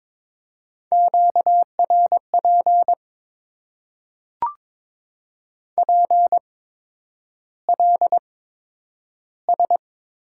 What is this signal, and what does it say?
Morse code, heard as a keyed tone, about 22 wpm, spaced as for 9 wpm, 700 Hz. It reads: QRP PLS
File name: Top 90 QSO Elements - 8x Character Spacing - 22wpm